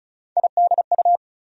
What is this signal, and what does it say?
Morse code, heard as a keyed tone, about 36 wpm, 700 Hz. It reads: IDU
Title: USA Call Sign Postfixes - 3 Letters 35wpm